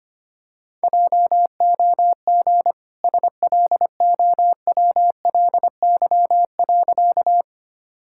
Morse code, heard as a keyed tone, about 25 wpm, 690 Hz. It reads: JOG SLOWLY.
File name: Easy Sentences with Hints - 25wpm